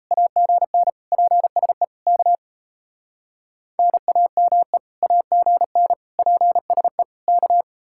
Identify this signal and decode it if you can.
Morse code, heard as a keyed tone, about 35 wpm, 700 Hz. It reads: AGN PSE K NAME AGN PSE K